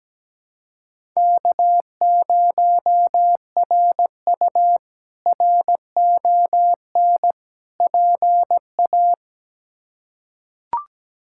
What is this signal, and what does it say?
Morse code, heard as a keyed tone, about 17 wpm, 700 Hz. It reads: K0RU RON PA
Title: K1USN SST QSO Practice - 17wpm